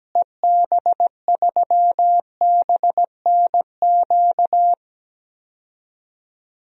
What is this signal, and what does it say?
Morse code, heard as a keyed tone, about 17 wpm, 700 Hz. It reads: EB3BNQ
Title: DXCC-100 Call Signs - 6 Characters - Repeated 2x - 17wpm